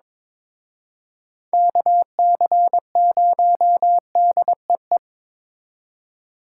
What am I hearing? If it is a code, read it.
Morse code, heard as a keyed tone, about 26 wpm, 700 Hz. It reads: KC0DEE
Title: Speed Racing - USA 2x3 Call Signs 22wpm